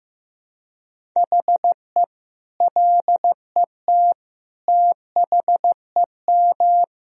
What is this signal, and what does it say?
Morse code, heard as a keyed tone, about 15 wpm, 700 Hz. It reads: HE LET THEM
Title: Sentences from Top 1000 Words - Review Part 3 - 15wpm